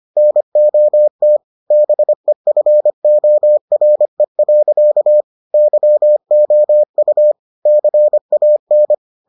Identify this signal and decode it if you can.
Morse code, heard as a keyed tone, about 25 wpm, 600 Hz. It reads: NOT BEFORE. YOU CAN